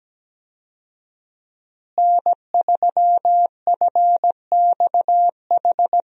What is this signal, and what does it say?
Morse code, heard as a keyed tone, about 17 wpm, 700 Hz. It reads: N3FXH
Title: USA Call Signs - 5 Characters - Repeated 2x - 17wpm